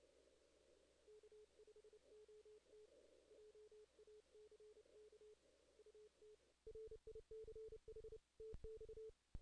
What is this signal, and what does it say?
Morse code, heard as a keyed tone, about 28 wpm, 440 Hz. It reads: K5OT JACK UT RICH TX
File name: 2016 JAN CW NAQP / 2016 Jan naqp k5ot - 0-9